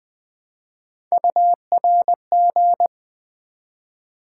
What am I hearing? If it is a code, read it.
Morse code, heard as a keyed tone, about 21 wpm, 700 Hz. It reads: URG
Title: Build A Word - 1 Letter At A Time - 4-Letter Words - 20wpm